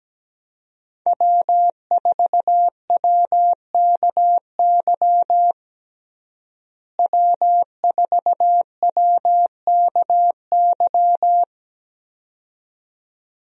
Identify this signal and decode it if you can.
Morse code, heard as a keyed tone, about 17 wpm, 700 Hz. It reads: W4WKY W4WKY
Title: USA Call Signs - 5 Characters - Repeated 2x - 17wpm